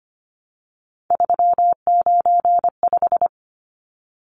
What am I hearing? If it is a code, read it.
Morse code, heard as a keyed tone, about 25 wpm, 700 Hz. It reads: 395
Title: Numbers - 3-Digit 25wpm